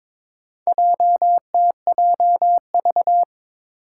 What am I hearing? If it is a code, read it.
Morse code, heard as a keyed tone, about 21 wpm, 700 Hz. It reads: JTJV
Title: Four Challenging Letter-Number Combinations 22wpm